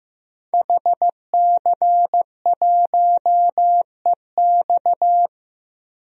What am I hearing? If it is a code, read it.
Morse code, heard as a keyed tone, about 16 wpm, 700 Hz. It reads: HC1EX
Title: DXCC-100 Call Signs - 5 Characters - 15wpm